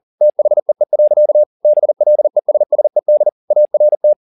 Morse code, heard as a keyed tone, about 40 wpm, 600 Hz. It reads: THEE. BLESSED ART